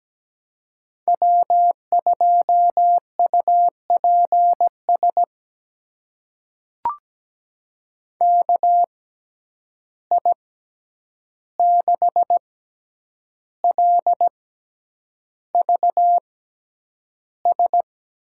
Morse code, heard as a keyed tone, about 17 wpm, spaced as for 8 wpm, 700 Hz. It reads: W2UPS KI6LVS
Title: USA Call Signs - 6x Character Spacing - 17wpm